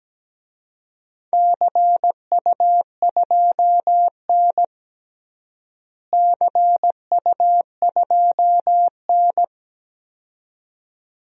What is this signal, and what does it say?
Morse code, heard as a keyed tone, about 17 wpm, 700 Hz. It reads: CU2N CU2N
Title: DXCC-100 Call Signs - 4 Characters - Repeated 2x - 17wpm